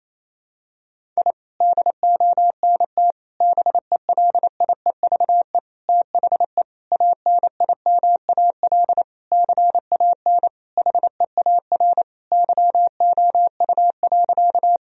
Morse code, heard as a keyed tone, about 28 wpm, 700 Hz. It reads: I DONT BELIEVE THE ANIMAL CAN HEAR YOU.
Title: Sentences from Top 1000 Words - Review Part 1 - 28wpm